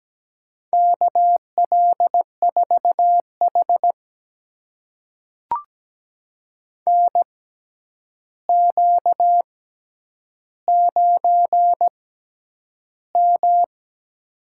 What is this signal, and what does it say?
Morse code, heard as a keyed tone, about 17 wpm, spaced as for 9 wpm, 700 Hz. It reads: KL4H NQ9M
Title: USA Call Signs - 4 Characters - 6x Character Spacing - 17wpm